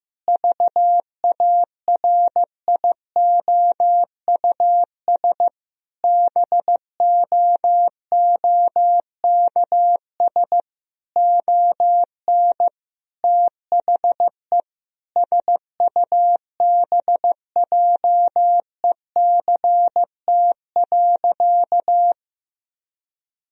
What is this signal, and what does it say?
Morse code, heard as a keyed tone, about 15 wpm, 700 Hz. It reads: VARIOUS BOOKS ON THE SUBJECT.
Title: Sentences from Top 1000 Words - Review Part 4 - 15wpm